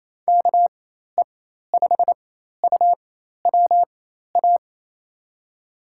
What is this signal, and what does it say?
Morse code, heard as a keyed tone, about 28 wpm, spaced as for 13 wpm, 700 Hz. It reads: KE5UWA
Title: USA Call Signs - 6 Characters - 4x Character Spacing - 28wpm